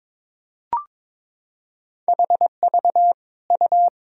Morse code, heard as a keyed tone, about 22 wpm, 700 Hz. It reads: HV U
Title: Sentences from Top 400 Words and Abbreviations 22wpm